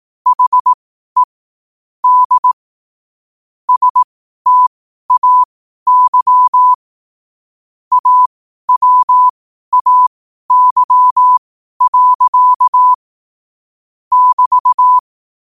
Morse code, heard as a keyed tone, about 18 wpm, spaced as for 12 wpm, 1000 Hz. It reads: HE D STAY AWAY. <BT>
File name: qotdfiles / 12wpm